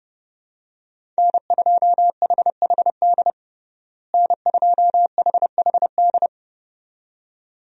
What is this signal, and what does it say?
Morse code, heard as a keyed tone, about 30 wpm, 700 Hz. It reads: N2HHD N2HHD
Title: USA Call Signs - 5 Characters - Repeated 2x - 30wpm